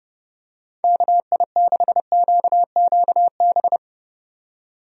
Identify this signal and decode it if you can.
Morse code, heard as a keyed tone, about 30 wpm, 700 Hz. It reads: KI6QQB
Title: USA Call Signs - 6 Characters - 30wpm